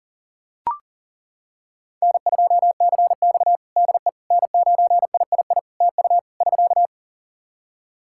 Morse code, heard as a keyed tone, about 37 wpm, 700 Hz. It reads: N2CX DE N9III TU <SK>
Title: K1USN SST QSO Practice 40wpm